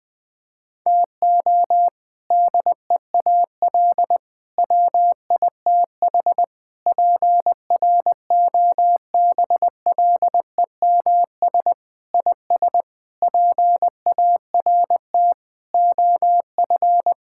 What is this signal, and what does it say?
Morse code, heard as a keyed tone, about 20 wpm, 700 Hz. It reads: TO DEAL WITH PROBLEMS IS PART OF
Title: Sentences with 3-Word Phrases - 20wpm